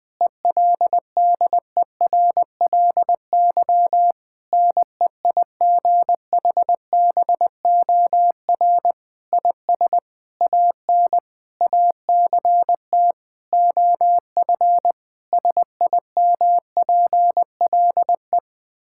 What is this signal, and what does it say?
Morse code, heard as a keyed tone, about 20 wpm, 700 Hz. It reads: ELDERLY NEIGHBOR IS AN ACT OF SIMPLE
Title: Difficult Sentences with Hints - 20wpm